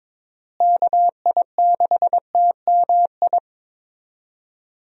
Morse code, heard as a keyed tone, about 22 wpm, 700 Hz. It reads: KI6TMI
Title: USA Call Signs - 6 Characters - Repeated 3x - 22wpm